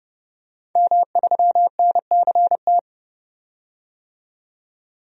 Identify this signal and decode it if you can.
Morse code, heard as a keyed tone, about 30 wpm, 700 Hz. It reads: M3NCT
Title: DXCC-100 Call Signs - 5 Characters - 30wpm